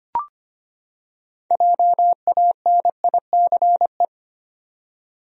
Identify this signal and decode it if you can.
Morse code, heard as a keyed tone, about 25 wpm, 700 Hz. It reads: JANICE
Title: USA Common Names - 25wpm